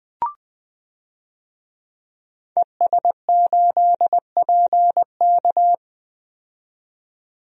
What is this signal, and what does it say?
Morse code, heard as a keyed tone, about 20 wpm, 700 Hz. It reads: ES8PK
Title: DXCC-100 Call Signs - 5 Characters - 20wpm